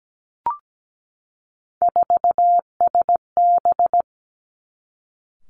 Morse code, heard as a keyed tone, about 17 wpm, 700 Hz. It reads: 4SB
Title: Three Challenging Letter-Number Combinations 17wpm